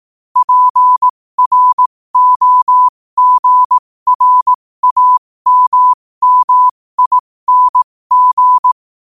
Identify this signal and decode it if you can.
Morse code, heard as a keyed tone, about 18 wpm, spaced as for 15 wpm, 1000 Hz. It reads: PROGRAMMING
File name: qotdfiles / 15wpm